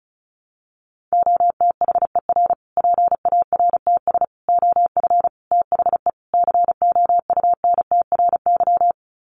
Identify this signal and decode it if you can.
Morse code, heard as a keyed tone, about 34 wpm, 700 Hz. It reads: OTHER PARTS OF THE COUNTRY
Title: Sets of 5 Words Encore 35wpm